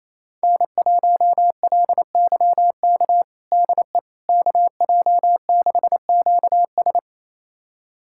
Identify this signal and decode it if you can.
Morse code, heard as a keyed tone, about 28 wpm, 700 Hz. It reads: N1LYK DE KJ6QS
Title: USA Callsign DE Callsign - 5 Characters - 28wpm